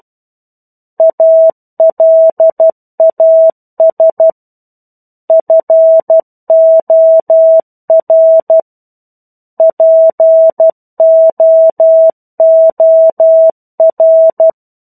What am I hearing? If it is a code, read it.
Morse code, heard as a keyed tone, about 12 wpm, 650 Hz. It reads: ALAS FOR POOR
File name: 01-Chapter-01 12 WPM